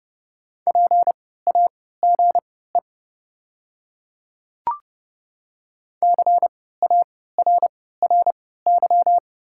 Morse code, heard as a keyed tone, about 30 wpm, spaced as for 17 wpm, 690 Hz. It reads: PAGE CARRY
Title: 3x Character Spacing - Top 300 Words - 30wpm